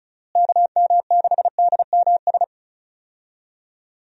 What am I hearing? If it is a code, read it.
Morse code, heard as a keyed tone, about 34 wpm, 700 Hz. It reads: KM6DMS
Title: USA Call Signs - 6 Characters - Repeated 3x - 35wpm